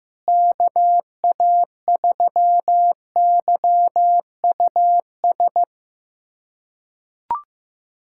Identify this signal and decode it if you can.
Morse code, heard as a keyed tone, about 16 wpm, 700 Hz. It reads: KA3YUS
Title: USA Callsign DE Callsign - 6 Characters - 15wpm